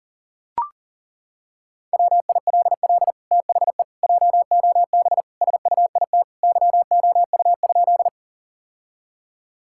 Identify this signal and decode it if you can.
Morse code, heard as a keyed tone, about 40 wpm, 700 Hz. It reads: WILL THE JOB SUIT YOU?
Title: Sentences from Top 1600 Words 40wpm